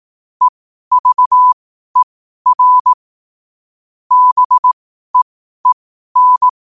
Morse code, heard as a keyed tone, about 18 wpm, spaced as for 12 wpm, 1000 Hz. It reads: EVER BEEN
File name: qotdfiles / 12wpm